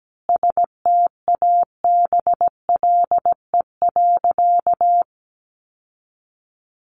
Morse code, heard as a keyed tone, about 17 wpm, 700 Hz. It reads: STABLE.
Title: Sentences from Top 1500 Words 17wpm